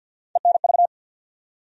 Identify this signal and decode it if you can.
Morse code, heard as a keyed tone, about 47 wpm, 700 Hz. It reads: ENV
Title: USA Call Sign Postfixes - 3 Letters 50wpm